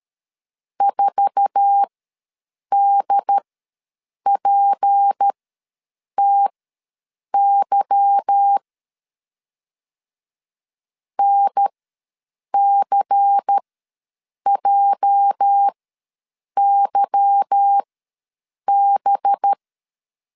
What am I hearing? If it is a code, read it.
Morse code, heard as a keyed tone, about 13 wpm, spaced as for 7 wpm, 800 Hz. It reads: 4DPTY NCJYB
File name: LEZIONI / 8 YJ45 / 8 YJ4 ws 12wpm 3249